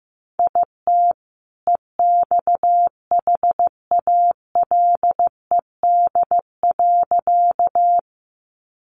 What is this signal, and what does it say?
Morse code, heard as a keyed tone, about 15 wpm, 700 Hz. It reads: IT EXHALED.
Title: A Sound Of Thunder 15wpm